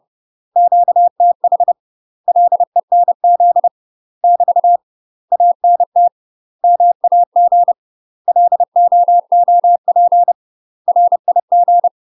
Morse code, heard as a keyed tone, about 30 wpm, 700 Hz. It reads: QTH LENZ <BT> ANT MAG LOOP RIG
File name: UK MASTER QSOs TRAINING/30WPM